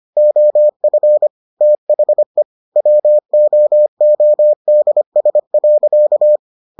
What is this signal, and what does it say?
Morse code, heard as a keyed tone, about 25 wpm, 600 Hz. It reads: OF THE WOODS.